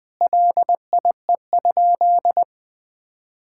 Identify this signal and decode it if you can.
Morse code, heard as a keyed tone, about 21 wpm, 700 Hz. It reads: LIE?
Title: Sentences from Top 1600 Words 20wpm